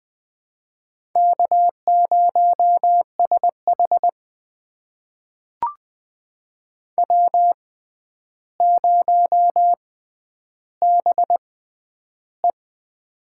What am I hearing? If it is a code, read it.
Morse code, heard as a keyed tone, about 20 wpm, spaced as for 10 wpm, 700 Hz. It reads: K0SH W0BE
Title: USA Call Signs - 4 Characters - 6x Character Spacing - 20wpm